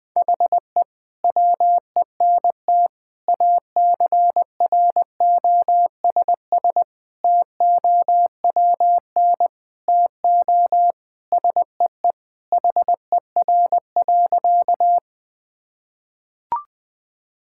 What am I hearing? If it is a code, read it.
Morse code, heard as a keyed tone, about 20 wpm, 700 Hz. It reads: HE WENT ACROSS TOWN TO SEE HER.
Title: Sentences from Top 1000 Words - Review Part 5 - 20wpm